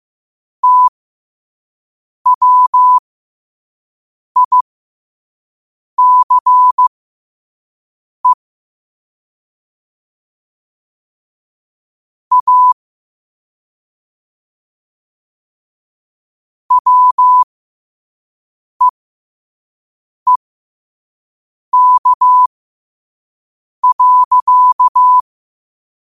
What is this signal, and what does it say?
Morse code, heard as a keyed tone, about 15 wpm, spaced as for 5 wpm, 1000 Hz. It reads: TWICE A WEEK.
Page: New quotes every day in morse code at 5 Words per minute.